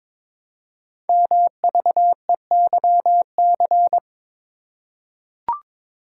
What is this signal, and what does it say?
Morse code, heard as a keyed tone, about 22 wpm, 700 Hz. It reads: MVEYC
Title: Code Groups - 5 Letters - 22wpm